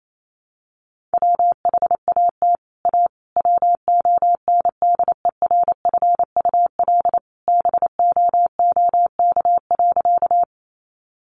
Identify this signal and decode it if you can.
Morse code, heard as a keyed tone, about 28 wpm, 700 Hz. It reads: WHAT A WONDERFUL BOOK.
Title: Sentences from Top 1100 Words 28wpm